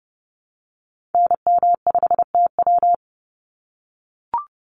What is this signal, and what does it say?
Morse code, heard as a keyed tone, about 30 wpm, 700 Hz. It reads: NM5TW
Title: NA SOTA Activator Callsigns 30wpm